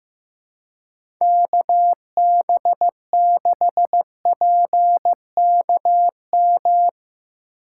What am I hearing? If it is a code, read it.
Morse code, heard as a keyed tone, about 15 wpm, 700 Hz. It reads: KB6PKM
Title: USA Call Signs - 8x Character Spacing - 15wpm